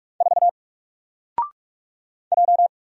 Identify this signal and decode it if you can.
Morse code, heard as a keyed tone, about 47 wpm, 700 Hz. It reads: 4 J
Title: Single Character (Letter:Number:Punctuation:Prosign) Alt 45wpm